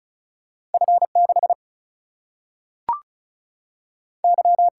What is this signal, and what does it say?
Morse code, heard as a keyed tone, about 36 wpm, 700 Hz. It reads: F6 Y
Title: Single-Letter Single-Number 35wpm